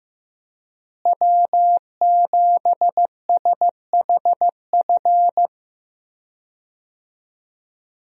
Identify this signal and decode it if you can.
Morse code, heard as a keyed tone, about 15 wpm, 700 Hz. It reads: W7SHF
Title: USA Call Signs - 5 Characters - Repeated 2x - 15wpm